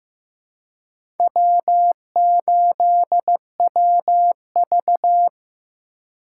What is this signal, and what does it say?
Morse code, heard as a keyed tone, about 15 wpm, 700 Hz. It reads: W8WV
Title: USA Call Signs - 4 Characters - 3x Character Spacing - 15wpm